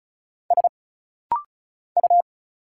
Morse code, heard as a keyed tone, about 36 wpm, 700 Hz. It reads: S U
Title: Single Letter-Number Alt - Rapid-Fire 35wpm